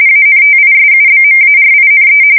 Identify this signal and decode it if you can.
RTTY, about 45.45 baud. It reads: ANTISUBMARINE
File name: fsk170 / usnavy4